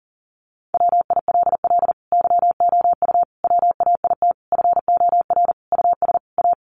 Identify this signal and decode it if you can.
Morse code, heard as a keyed tone, about 40 wpm, 700 Hz. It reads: WILL YOU WAIT FOR US A
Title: Sentences from Top 900 Words 40wpm